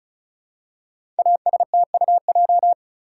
Morse code, heard as a keyed tone, about 33 wpm, 700 Hz. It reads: ASTUJ
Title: Code Groups - 5 Letters - 35wpm